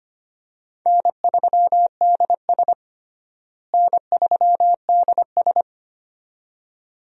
Morse code, heard as a keyed tone, about 25 wpm, 700 Hz. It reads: N3DS N3DS
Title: USA Call Signs - 4 Characters - Repeated 2x - 25wpm